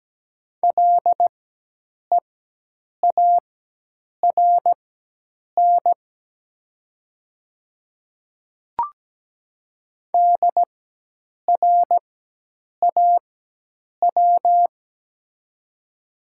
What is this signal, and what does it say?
Morse code, heard as a keyed tone, about 17 wpm, spaced as for 8 wpm, 700 Hz. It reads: LEARN DRAW
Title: 4x Character Spacing - Top 500 Words - 17wpm